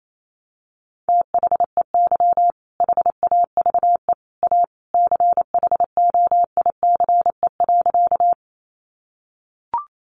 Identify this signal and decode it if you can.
Morse code, heard as a keyed tone, about 28 wpm, 700 Hz. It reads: THEY HAVE A CHOICE.
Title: Sentences from Top 900 Words 28wpm